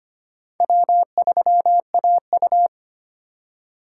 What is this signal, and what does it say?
Morse code, heard as a keyed tone, about 25 wpm, 700 Hz. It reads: W3AU
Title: USA Call Signs - 4 Characters - 3x Character Spacing - 25wpm